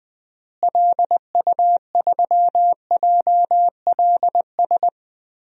Morse code, heard as a keyed tone, about 21 wpm, 700 Hz. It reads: LU3JLS
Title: DXCC-100 Call Signs - 6 Characters - Repeated 3x - 20wpm